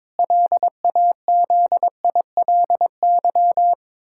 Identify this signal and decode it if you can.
Morse code, heard as a keyed tone, about 22 wpm, 700 Hz. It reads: LAZILY
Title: Pangram Challenge v2 - 22wpm